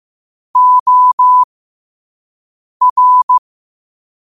Morse code, heard as a keyed tone, about 15 wpm, spaced as for 5 wpm, 1000 Hz. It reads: OR
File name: qotdfiles / 5wpm